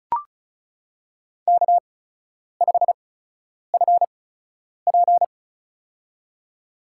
Morse code, heard as a keyed tone, about 36 wpm, spaced as for 10 wpm, 700 Hz. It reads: K5FP
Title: USA Call Signs - 8x Character Spacing - 35wpm